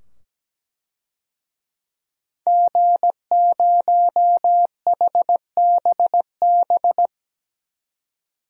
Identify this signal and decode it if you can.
Morse code, heard as a keyed tone, about 17 wpm, 700 Hz. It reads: G0HBB
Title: DXCC-100 Call Signs - 5 Characters - 17wpm